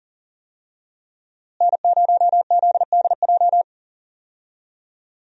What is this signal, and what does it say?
Morse code, heard as a keyed tone, about 40 wpm, 700 Hz. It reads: N0ZDJ
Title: USA Call Signs - 5 Characters - Repeated 2x - 40wpm